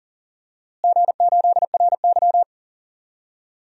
Morse code, heard as a keyed tone, about 37 wpm, 700 Hz. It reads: G8RY
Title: DXCC-100 Call Signs - 4 Characters - Repeated 2x - 40wpm